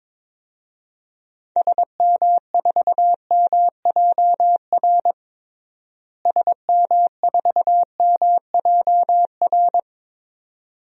Morse code, heard as a keyed tone, about 22 wpm, 700 Hz. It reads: SM4MJR SM4MJR
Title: DXCC-100 Call Signs - 6 Characters - Repeated 2x - 22wpm